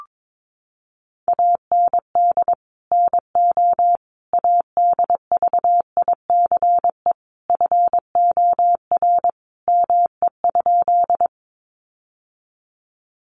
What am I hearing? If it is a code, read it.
Morse code, heard as a keyed tone, about 22 wpm, 700 Hz. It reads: AND NO ADVICE FOR ME?
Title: Sentences from Top 1100 Words 22wpm